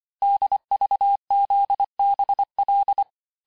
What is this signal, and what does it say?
Morse code, heard as a keyed tone, about 25 wpm, 800 Hz. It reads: DVZBL
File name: texte-chiffre-vitesse-23mpm